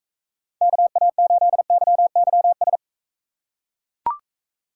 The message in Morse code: KA8YYS

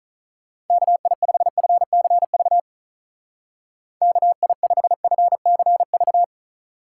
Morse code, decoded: KI5FCV KI5FCV

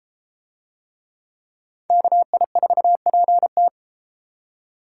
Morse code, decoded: KI4PT

